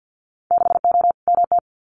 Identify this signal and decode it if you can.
Morse code, heard as a keyed tone, about 50 wpm, 700 Hz. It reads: 6O MT
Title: Winter Field Day Exchanges Encore 50wpm